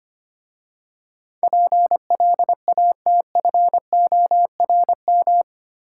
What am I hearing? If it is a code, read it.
Morse code, heard as a keyed tone, about 25 wpm, 700 Hz. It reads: PLATFORM
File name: practice / 25wpm / Easy Words with Hints - 25wpm